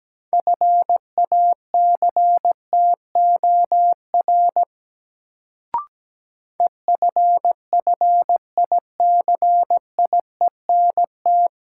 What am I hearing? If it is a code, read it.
Morse code, heard as a keyed tone, about 17 wpm, 700 Hz. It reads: FACTOR EFFICIENT
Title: Sentences from Top 1300 Words - New Word Drill Alt - 17wpm